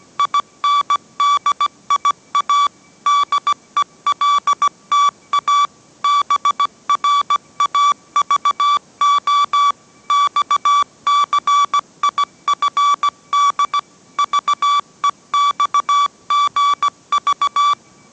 Morse code, heard as a keyed tone, about 17 wpm, 1150 Hz. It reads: INDIA DELTA BRAVO XCIFD VEXGV